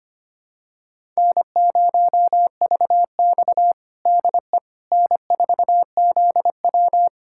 Morse code, heard as a keyed tone, about 25 wpm, 700 Hz. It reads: N0VX DE N4ZW
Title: USA Callsign DE Callsign - 4 Characters - 25wpm